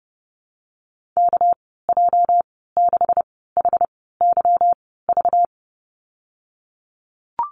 Morse code, heard as a keyed tone, about 30 wpm, spaced as for 17 wpm, 700 Hz. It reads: KJ6HYV
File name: Lesson 37g - Letter x - Call Signs 30wpm